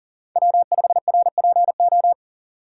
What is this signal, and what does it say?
Morse code, heard as a keyed tone, about 40 wpm, 700 Hz. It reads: W5RPO
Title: USA Call Signs - 5 Characters - 6x Character Spacing - 40wpm